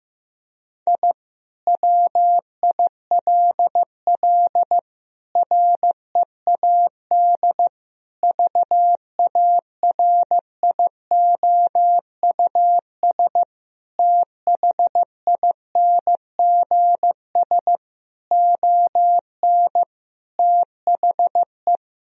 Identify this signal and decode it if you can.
Morse code, heard as a keyed tone, about 15 wpm, 700 Hz. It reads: I WILL READ VARIOUS THINGS ON THE
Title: Sentences from Top 1000 Words - Review Part 4 - 15wpm